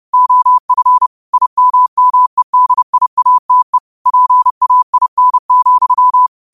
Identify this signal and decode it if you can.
Morse code, heard as a keyed tone, about 30 wpm, 1000 Hz. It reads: OF IMMEDIATE PAIN,